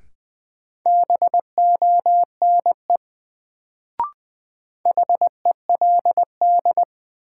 Morse code, heard as a keyed tone, about 20 wpm, 700 Hz. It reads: BONE HELD
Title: Sentences from Top 1600 Words - New Word Drill - 20wpm